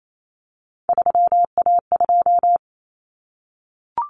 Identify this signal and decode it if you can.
Morse code, heard as a keyed tone, about 28 wpm, 700 Hz. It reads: 3A2
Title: Common Cut Numbers - 3-Digit 28wpm